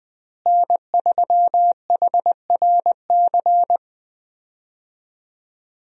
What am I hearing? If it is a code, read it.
Morse code, heard as a keyed tone, about 21 wpm, 700 Hz. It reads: N3HRC
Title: USA Call Signs - 5 Characters - Repeated 3x - 20wpm